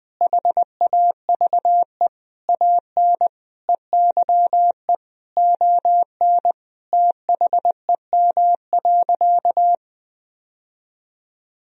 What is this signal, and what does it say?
Morse code, heard as a keyed tone, about 20 wpm, 700 Hz. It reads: HAVE AN EYE ON THEM.
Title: Sentences from Top 1000 Words - Review Part 1 - 20wpm